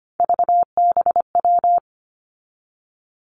Morse code, heard as a keyed tone, about 25 wpm, 700 Hz. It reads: VBW
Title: Three Challenging Letter-Number Combinations 25wpm